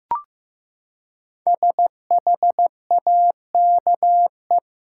Morse code, heard as a keyed tone, about 16 wpm, 700 Hz. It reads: SHAKE
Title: Five Letter Words - 15wpm